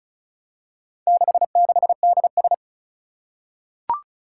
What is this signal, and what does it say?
Morse code, heard as a keyed tone, about 36 wpm, 700 Hz. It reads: 66DS
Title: Four Challenging Letter-Number Combinations 35wpm